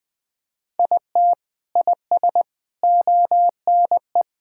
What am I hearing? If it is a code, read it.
Morse code, heard as a keyed tone, about 20 wpm, 700 Hz. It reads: IT IS ONE
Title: Sentences with 4-Word Phrases - 20wpm